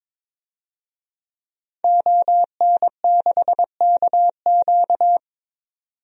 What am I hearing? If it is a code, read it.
Morse code, heard as a keyed tone, about 22 wpm, 700 Hz. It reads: ON6KQ
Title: DXCC-100 Call Signs - 5 Characters - 22wpm